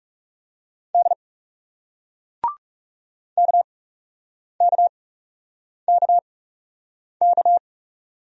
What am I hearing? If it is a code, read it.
Morse code, heard as a keyed tone, about 40 wpm, spaced as for 8 wpm, 700 Hz. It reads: D KKKK